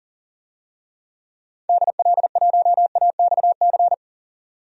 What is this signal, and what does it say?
Morse code, heard as a keyed tone, about 40 wpm, 700 Hz. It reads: DL1AXC
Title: DXCC-100 Call Signs - 6 Characters - 40wpm